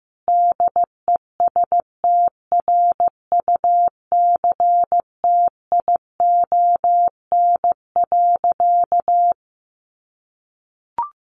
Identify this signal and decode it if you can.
Morse code, heard as a keyed tone, about 15 wpm, 700 Hz. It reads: DESTRUCTION.
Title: A Sound Of Thunder 15wpm